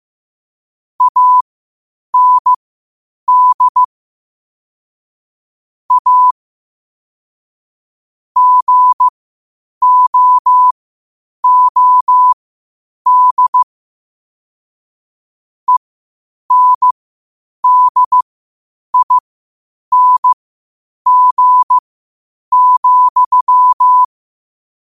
Morse code, heard as a keyed tone, about 15 wpm, spaced as for 8 wpm, 1000 Hz. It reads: AND A GOOD ENDING,
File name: qotdfiles / 8wpm